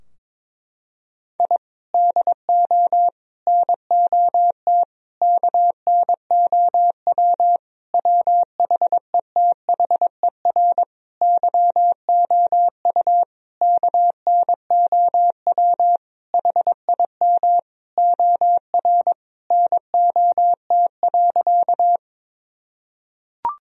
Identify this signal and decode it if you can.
Morse code, heard as a keyed tone, about 22 wpm, 700 Hz. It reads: I DO NOT KNOW WHETHER YOU KNOW HIM OR NOT.
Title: Sentences from Top 1000 Words - Review Part 3 - 22wpm